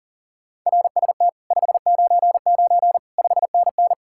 Morse code, heard as a keyed tone, about 50 wpm, 700 Hz. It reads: RST 599 5NN
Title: Speed-Racing Common QSO Phrases Encore - 40wpm